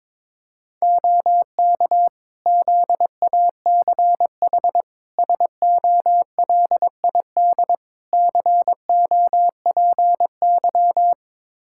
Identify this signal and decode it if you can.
Morse code, heard as a keyed tone, about 22 wpm, 700 Hz. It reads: OK ZACH SOLID COPY
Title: K1USN SST QSO Practice 22wpm